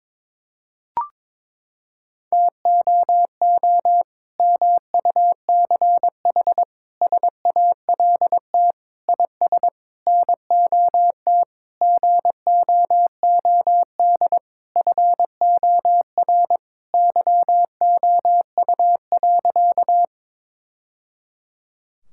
Morse code, heard as a keyed tone, about 22 wpm, 700 Hz. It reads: TOO MUCH SALT IS NOT GOOD FOR YOU.
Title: Sentences from Top 1000 Words - Review Part 5 - 22wpm